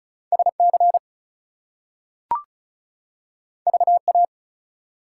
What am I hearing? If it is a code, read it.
Morse code, heard as a keyed tone, about 36 wpm, 700 Hz. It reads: SC VA
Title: US States and Canadian Provinces 35wpm